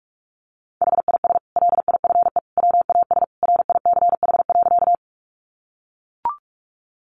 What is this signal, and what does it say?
Morse code, heard as a keyed tone, about 44 wpm, 700 Hz. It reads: HIS LIFE WAS RICH.
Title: Sentences from Top 1100 Words 45wpm